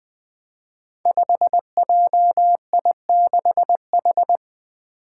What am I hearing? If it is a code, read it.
Morse code, heard as a keyed tone, about 20 wpm, 700 Hz. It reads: 5JI6H
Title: Five Challenging Letter-Number Combinations 20wpm